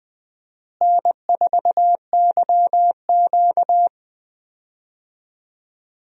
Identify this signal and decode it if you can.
Morse code, heard as a keyed tone, about 20 wpm, 700 Hz. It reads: N4YQ
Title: USA Call Signs - 4 Characters - Repeated 2x - 20wpm